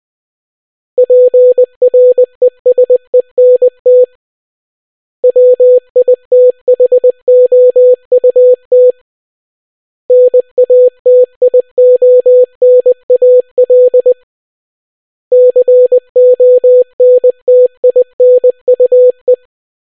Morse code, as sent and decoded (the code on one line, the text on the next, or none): .--. .-. ... -. - / .-- .. - .... --- ..- - / -. .- - .. --- -. .- .-.. / -.-. --- -. - .. -. ..- .
PRESENT WITHOUT NATIONAL CONTINUE